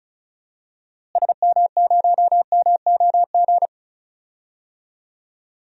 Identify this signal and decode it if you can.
Morse code, heard as a keyed tone, about 36 wpm, 700 Hz. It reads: SM0MOG
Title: DXCC-100 Call Signs - 6 Characters - Repeated 2x - 35wpm